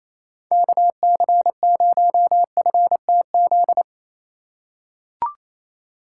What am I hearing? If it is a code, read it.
Morse code, heard as a keyed tone, about 27 wpm, 700 Hz. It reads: KC0FTZ
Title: USA Call Signs - 6 Characters - 3x Character Spacing - 28wpm